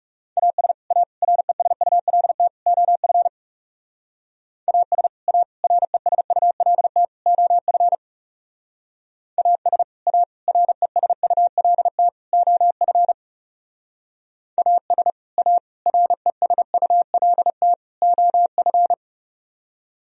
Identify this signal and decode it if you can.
Morse code, heard as a keyed tone, about 38 wpm, 700 Hz. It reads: AS A RESULT OF AS A RESULT OF AS A RESULT OF AS A RESULT OF